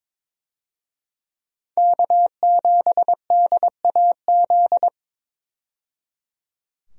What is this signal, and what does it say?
Morse code, heard as a keyed tone, about 22 wpm, 700 Hz. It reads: K7DAZ
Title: USA Call Signs - 5 Characters - 22wpm